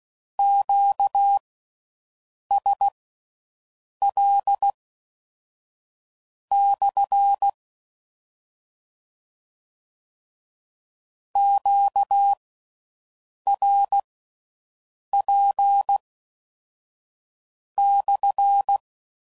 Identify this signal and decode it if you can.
Morse code, heard as a keyed tone, about 16 wpm, spaced as for 5 wpm, 800 Hz. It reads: QSL/ QRP/